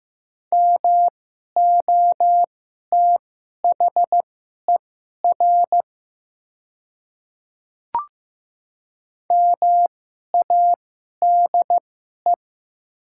Morse code, heard as a keyed tone, about 15 wpm, spaced as for 11 wpm, 700 Hz. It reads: MOTHER MADE
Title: 2x Character Spacing - Top 200 Words - 15wpm